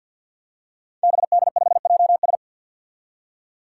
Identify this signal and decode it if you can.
Morse code, heard as a keyed tone, about 50 wpm, 700 Hz. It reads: BD5JS
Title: Five Challenging Letter-Number Combinations 50wpm